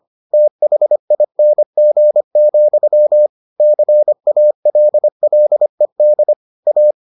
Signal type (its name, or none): Morse code